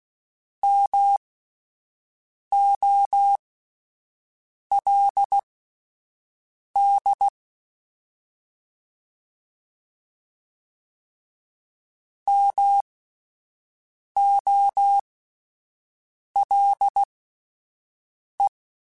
Morse code, heard as a keyed tone, about 16 wpm, spaced as for 5 wpm, 800 Hz. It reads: MOLD MOLE
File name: Alphabetical / Alphabetical mp3s / Ms @5